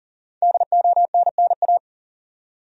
Morse code, heard as a keyed tone, about 40 wpm, 700 Hz. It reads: DONNA
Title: USA Common Names - Max 5 Letters - 40wpm